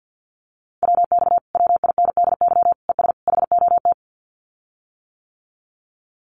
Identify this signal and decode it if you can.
Morse code, heard as a keyed tone, about 50 wpm, 700 Hz. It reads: WX WINDY ES HOT
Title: Common QSO Phrases Encore 50wpm